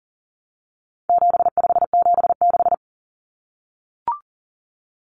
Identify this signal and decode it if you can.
Morse code, heard as a keyed tone, about 40 wpm, 700 Hz. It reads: 7576
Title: Common Cut Numbers - 4-Digit 40wpm